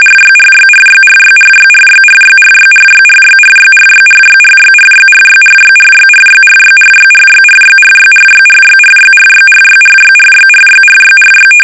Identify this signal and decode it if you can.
RTTY, about 45.45 baud. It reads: RYRYRYRYRYRYRYRYRYRYRYRYRYRYRYRYRYRYRYRYRYRYRYRYRYRYRYRYRYRYRYRYRYRY
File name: rtty / RTTY Test 850